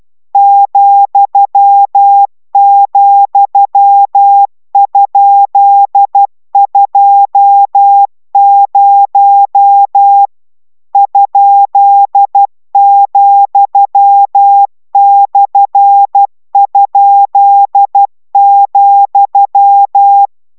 Morse code, heard as a keyed tone, about 12 wpm, 800 Hz. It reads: ,,?20 ?,/?,